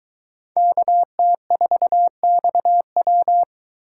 Morse code, heard as a keyed tone, about 23 wpm, 700 Hz. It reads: KT4XW